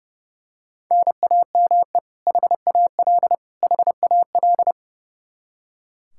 Morse code, heard as a keyed tone, about 32 wpm, 700 Hz. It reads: NAME HAL HAL
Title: Practice QSO - Exchange 1 - Beginning - 30wpm